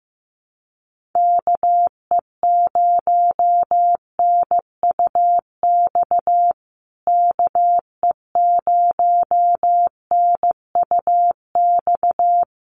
Morse code, heard as a keyed tone, about 15 wpm, 700 Hz. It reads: KE0NUX KE0NUX
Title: USA Call Signs Repeated 3x 15wpm